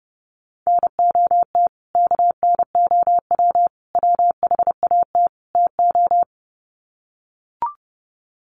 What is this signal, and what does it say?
Morse code, heard as a keyed tone, about 30 wpm, 700 Hz. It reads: NOT KNOW WHAT TO
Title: Sets of 4 Words 30wpm